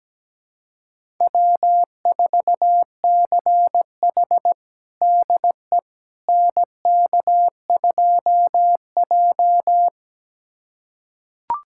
Morse code, heard as a keyed tone, about 17 wpm, 700 Hz. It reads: W4CH DE NK2J
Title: USA Callsign DE Callsign - 4 Characters - 17wpm